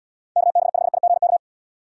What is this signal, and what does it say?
Morse code, heard as a keyed tone, about 78 wpm, 700 Hz. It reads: KB5IGK